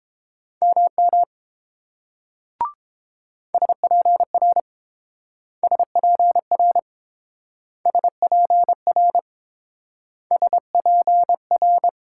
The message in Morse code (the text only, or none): MM SPR SPR SPR SPR